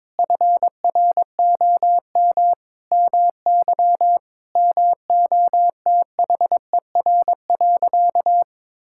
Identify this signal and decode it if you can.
Morse code, heard as a keyed tone, about 22 wpm, 700 Hz. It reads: FROM MY MOTHER.
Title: Sentences from Top 1000 Words - Review Part 6 - 22wpm